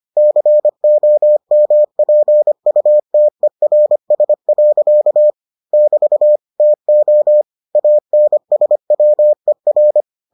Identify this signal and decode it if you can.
Morse code, heard as a keyed tone, about 25 wpm, 600 Hz. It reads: COMPUTERS. <BT> TO ANSWER